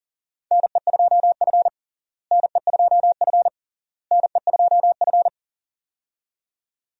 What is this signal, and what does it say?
Morse code, heard as a keyed tone, about 37 wpm, 700 Hz. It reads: NE2F NE2F NE2F
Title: USA Call Signs - 4 Characters - Repeated 3x - 40wpm